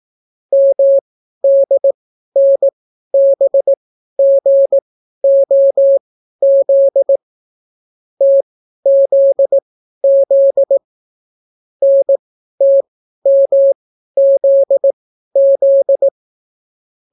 Morse code, heard as a keyed tone, about 18 wpm, spaced as for 12 wpm, 550 Hz. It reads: MDNBGOZ TZZ NTMZZ